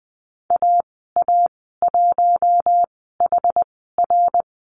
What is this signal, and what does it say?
Morse code, heard as a keyed tone, about 20 wpm, spaced as for 15 wpm, 700 Hz. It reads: AA1HR
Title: 076 Lesson 13g - Letter L - Call Signs 20wpm